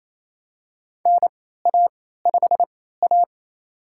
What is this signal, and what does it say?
Morse code, heard as a keyed tone, about 27 wpm, spaced as for 16 wpm, 700 Hz. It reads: NA5A
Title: USA Call Signs - 4 Characters - 3x Character Spacing - 28wpm